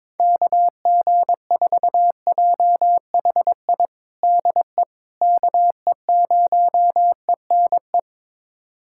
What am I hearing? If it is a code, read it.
Morse code, heard as a keyed tone, about 22 wpm, 700 Hz. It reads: KG4JHI DE KE0ENE
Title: USA Callsign DE Callsign 22wpm